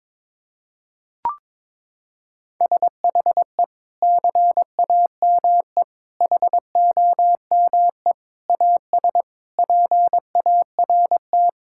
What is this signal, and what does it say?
Morse code, heard as a keyed tone, about 22 wpm, 700 Hz. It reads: SHE CAME HOME AS PART
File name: practice / 22wpm / Sentences with 4-Word Phrases - 22wpm